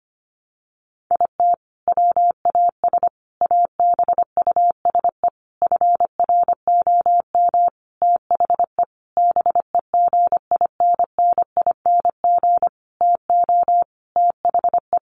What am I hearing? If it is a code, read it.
Morse code, heard as a keyed tone, about 25 wpm, 700 Hz. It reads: IT WAS ABUSE FROM THE BEGINNING TO THE
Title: Sentences from Top 1300 Words 25wpm